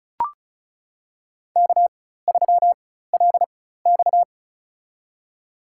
Morse code, heard as a keyed tone, about 36 wpm, spaced as for 16 wpm, 700 Hz. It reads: K3LX
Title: USA Call Signs - 4 Characters - 4x Character Spacing - 35wpm